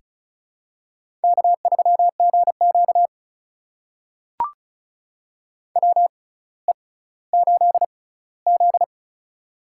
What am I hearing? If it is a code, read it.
Morse code, heard as a keyed tone, about 36 wpm, spaced as for 18 wpm, 700 Hz. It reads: K3GQ WE8Z
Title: USA Call Signs - 6x Character Spacing - 35wpm